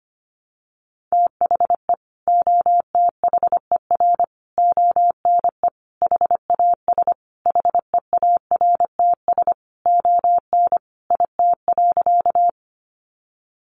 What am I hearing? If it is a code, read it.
Morse code, heard as a keyed tone, about 25 wpm, 700 Hz. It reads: THE OTHER ONE HAS HEARTS ON IT.
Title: Additional Easy Phrases 25wpm